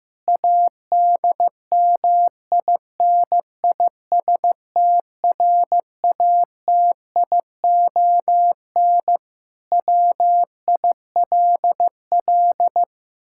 Morse code, heard as a keyed tone, about 15 wpm, 700 Hz. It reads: ADMINISTRATION WILL
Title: Sentences from Top 1600 Words 15wpm